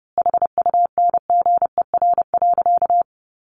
Morse code, heard as a keyed tone, about 30 wpm, 700 Hz. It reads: HUNGER.
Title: A Sound Of Thunder 30wpm